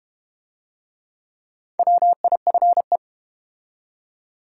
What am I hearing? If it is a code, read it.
Morse code, heard as a keyed tone, about 30 wpm, 700 Hz. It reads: WIFE
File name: Easy Words - Speed-Racing with Hints - 25wpm